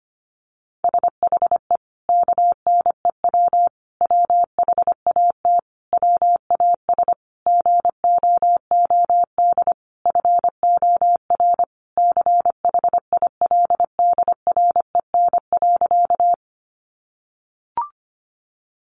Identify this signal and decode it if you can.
Morse code, heard as a keyed tone, about 25 wpm, 700 Hz. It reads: SHE KNEW WHAT WAS GOOD FOR CHILDREN.
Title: Sentences from Top 400 Words 25wpm